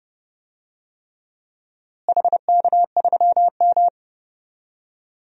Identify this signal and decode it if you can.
Morse code, heard as a keyed tone, about 30 wpm, 700 Hz. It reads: HK3M
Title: DXCC-100 Call Signs - 4 Characters - Repeated 3x - 30wpm